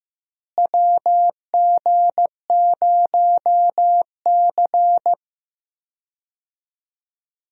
Morse code, heard as a keyed tone, about 15 wpm, 700 Hz. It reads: WG0C
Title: USA Call Signs - 4 Characters - Repeated 2x - 15wpm